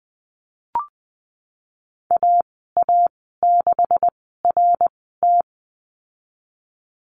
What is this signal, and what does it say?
Morse code, heard as a keyed tone, about 20 wpm, spaced as for 15 wpm, 700 Hz. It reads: AA6RT
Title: 128 Lesson 21g - Number 6 - Call Signs 20wpm